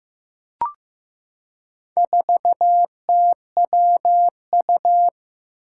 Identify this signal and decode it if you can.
Morse code, heard as a keyed tone, about 15 wpm, 700 Hz. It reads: 4TWU
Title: Four Challenging Letter-Number Combinations 15wpm